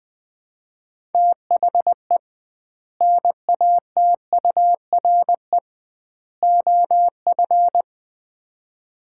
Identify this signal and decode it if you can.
Morse code, heard as a keyed tone, about 20 wpm, 700 Hz. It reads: THE NATURE OF
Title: 2x Word-Spacing - Sets of 3 Words - 20wpm